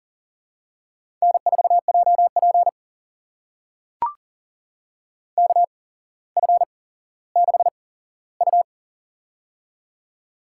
Morse code, heard as a keyed tone, about 40 wpm, spaced as for 17 wpm, 700 Hz. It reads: N4JP KF6U
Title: USA Call Signs - 4 Characters - 8x Character Spacing - 40wpm